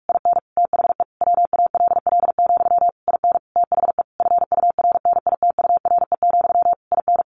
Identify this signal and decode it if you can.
Morse code, heard as a keyed tone, about 44 wpm, 700 Hz. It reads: IN THE WALL, IN THE FURNITURE, IN
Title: A Sound Of Thunder 45wpm